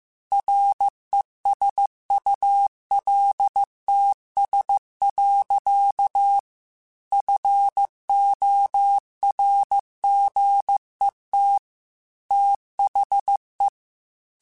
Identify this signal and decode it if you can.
Morse code, heard as a keyed tone, about 15 wpm, 800 Hz. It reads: RESULTS. FORGET THE
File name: Technical Subjects / Technical Subjects-mp3s / Gates Intro @13